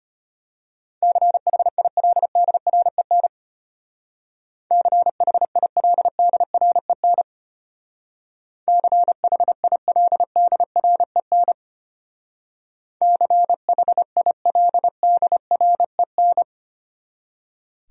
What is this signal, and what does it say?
Morse code, heard as a keyed tone, about 31 wpm, 700 Hz. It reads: CHILDREN CHILDREN CHILDREN CHILDREN
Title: Speed-Racing Top 500 Words 25wpm